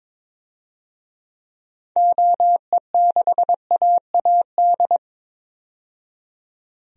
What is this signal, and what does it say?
Morse code, heard as a keyed tone, about 21 wpm, 700 Hz. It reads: OE6AAD
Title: DXCC-100 Call Signs - 6 Characters - 22wpm